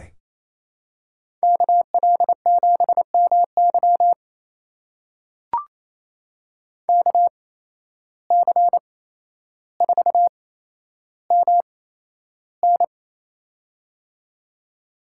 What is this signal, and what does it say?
Morse code, heard as a keyed tone, about 28 wpm, spaced as for 12 wpm, 700 Hz. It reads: KL7MY KC4MN
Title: USA Call Signs - 5 Characters - 8x Character Spacing - 28wpm